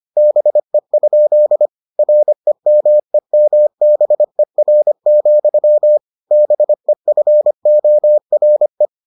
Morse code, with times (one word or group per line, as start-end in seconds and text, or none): 0.2-1.7 BE?
2.0-6.0 REMEMBER,
6.3-8.9 BEFORE